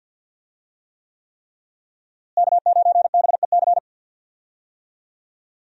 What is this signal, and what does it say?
Morse code, heard as a keyed tone, about 46 wpm, 700 Hz. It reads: K9BEC
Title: USA Call Signs - 5 Characters - Repeated 3x - 50wpm